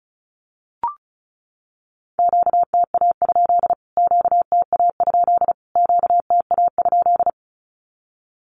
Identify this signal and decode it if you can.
Morse code, heard as a keyed tone, about 34 wpm, 700 Hz. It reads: QTA? QTA? QTA?
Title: Q-Codes Repeated 3x 35wpm